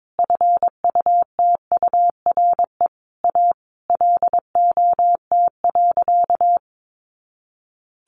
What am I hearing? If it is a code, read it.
Morse code, heard as a keyed tone, about 22 wpm, 700 Hz. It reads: FUTURE A LOT.
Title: Easy Phrases 22wpm